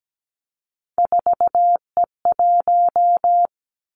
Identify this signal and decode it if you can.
Morse code, heard as a keyed tone, about 17 wpm, 700 Hz. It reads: 4E1
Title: Three Challenging Letter-Number Combinations 17wpm